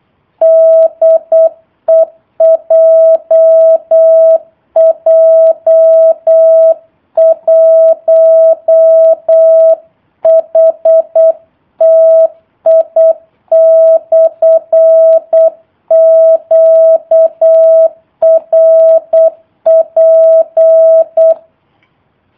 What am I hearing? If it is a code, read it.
Morse code, heard as a keyed tone, about 8 wpm, 650 Hz. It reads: DE JJ1HTI/QRP
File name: jh8cbh / 2019cw / sp03- w